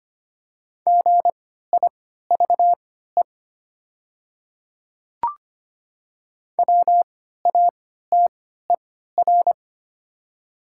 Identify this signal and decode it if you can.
Morse code, heard as a keyed tone, about 25 wpm, spaced as for 14 wpm, 700 Hz. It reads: GIVE WATER
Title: 3x Character Spacing - Top 300 Words - 25wpm